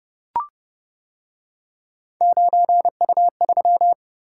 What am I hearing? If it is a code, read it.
Morse code, heard as a keyed tone, about 29 wpm, 700 Hz. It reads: 9U3